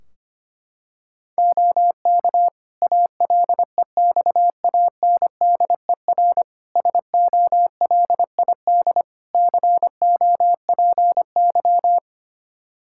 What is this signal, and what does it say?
Morse code, heard as a keyed tone, about 25 wpm, 700 Hz. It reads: OK ALEXANDER SOLID COPY
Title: K1USN SST QSO Practice 25wpm